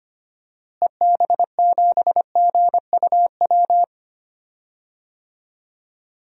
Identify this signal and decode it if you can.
Morse code, heard as a keyed tone, about 25 wpm, 700 Hz. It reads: EB7GUW
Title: DXCC-100 Call Signs - 6 Characters - Repeated 2x - 25wpm